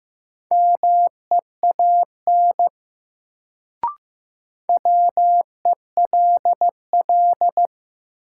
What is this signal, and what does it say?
Morse code, heard as a keyed tone, about 15 wpm, 700 Hz. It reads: MEAN WELL
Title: Sentences from Top 200 Words - New Word Drill - 15wpm